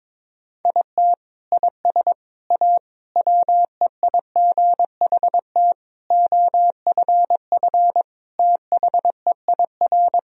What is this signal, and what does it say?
Morse code, heard as a keyed tone, about 21 wpm, 700 Hz. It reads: IT IS A WEIGHT OFF THEIR
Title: Sentences from Top 1000 Words - Review Part 3 - 22wpm